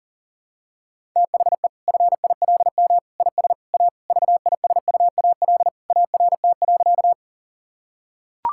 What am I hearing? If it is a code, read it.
Morse code, heard as a keyed tone, about 40 wpm, 700 Hz. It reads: THE FILM IS A VISUAL ART.
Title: Sentences from Top 1000 Words - Review Part 6 - 40wpm